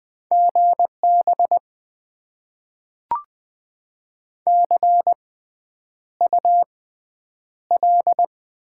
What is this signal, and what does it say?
Morse code, heard as a keyed tone, about 20 wpm, spaced as for 9 wpm, 700 Hz. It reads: GB CUL
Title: Top 90 QSO Elements - 6x Character Spacing - 20wpm